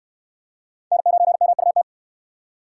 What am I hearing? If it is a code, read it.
Morse code, heard as a keyed tone, about 66 wpm, 700 Hz. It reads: N9MUT